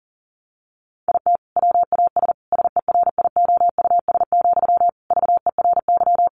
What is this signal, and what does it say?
Morse code, heard as a keyed tone, about 40 wpm, 700 Hz. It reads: IT WAS SERIOUS, VERY
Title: Sentences from Top 1100 Words 40wpm